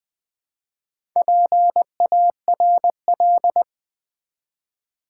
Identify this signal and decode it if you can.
Morse code, heard as a keyed tone, about 20 wpm, 700 Hz. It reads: PARL
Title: Build A Word - 4 Letters At A Time - 10-Letter Words - 20wpm